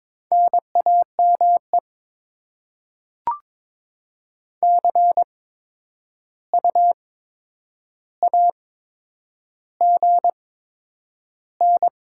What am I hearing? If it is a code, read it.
Morse code, heard as a keyed tone, about 22 wpm, spaced as for 9 wpm, 700 Hz. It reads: NAME CU AGN
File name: Top 20 QSO Elements - 8x Character Spacing - 22wpm